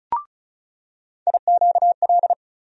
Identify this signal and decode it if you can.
Morse code, heard as a keyed tone, about 34 wpm, 700 Hz. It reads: IQL
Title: USA Call Sign Postfixes - 3 Letters 35wpm